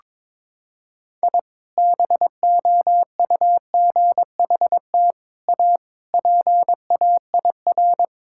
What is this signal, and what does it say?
Morse code, heard as a keyed tone, about 22 wpm, 700 Hz. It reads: I BOUGHT A PAIR
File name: practice / 22wpm / Sentences with 3-Word Phrases - 22wpm